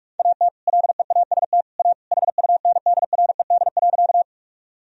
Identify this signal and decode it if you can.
Morse code, heard as a keyed tone, about 46 wpm, 700 Hz. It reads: AT LEAST A HUNDRED.
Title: Sentences from Top 1000 Words - Review Part 2 - 45wpm